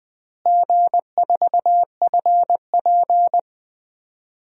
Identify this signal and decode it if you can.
Morse code, heard as a keyed tone, about 19 wpm, 700 Hz. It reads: G4FP
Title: DXCC-100 Call Signs - 4 Characters - Repeated 2x - 20wpm